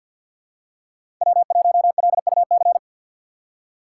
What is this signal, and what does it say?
Morse code, heard as a keyed tone, about 47 wpm, 700 Hz. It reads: W1LUC
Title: USA Call Signs - 4x Character Spacing - 50wpm